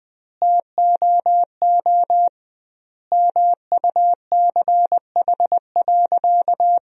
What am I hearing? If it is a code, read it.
Morse code, heard as a keyed tone, about 21 wpm, 700 Hz. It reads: TOO MUCH.
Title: 2x Word-Spacing - Sentences from Top 400 Words - 20wpm